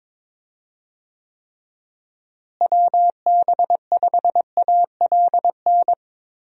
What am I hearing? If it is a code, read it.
Morse code, heard as a keyed tone, about 22 wpm, 700 Hz. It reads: WB5ALN
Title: USA Call Signs - 6 Characters - Repeated 2x - 22wpm